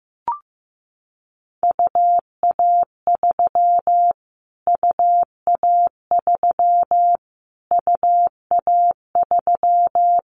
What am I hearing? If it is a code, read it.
Morse code, heard as a keyed tone, about 15 wpm, 700 Hz. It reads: UA3 UA3 UA3
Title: Callsign Prefixes for Top 25 DXCC - Repeated 3x - 15wpm